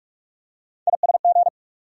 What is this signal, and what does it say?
Morse code, heard as a keyed tone, about 43 wpm, 700 Hz. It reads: ISG
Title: USA Call Sign Postfixes - 3 Letters 45wpm